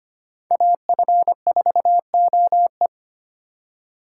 Morse code, heard as a keyed tone, about 25 wpm, 700 Hz. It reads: AF4OE
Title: USA Call Signs - 4x Character Spacing - 25wpm